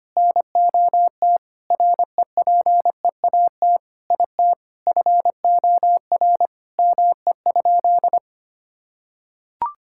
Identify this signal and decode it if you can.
Morse code, heard as a keyed tone, about 25 wpm, 700 Hz. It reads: NOT REPEAT IT FOR ME?
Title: Sentences from Top 1600 Words 25wpm